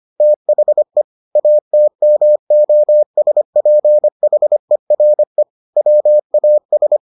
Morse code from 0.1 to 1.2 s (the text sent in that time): THE